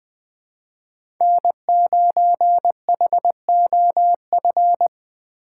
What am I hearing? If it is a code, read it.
Morse code, heard as a keyed tone, about 21 wpm, 700 Hz. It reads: N9HOF
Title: USA Call Signs - 5 Characters - 3x Character Spacing - 20wpm